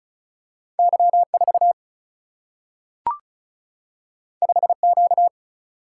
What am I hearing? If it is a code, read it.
Morse code, heard as a keyed tone, about 36 wpm, 700 Hz. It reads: Y4 5Q
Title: Single-Letter Single-Number 35wpm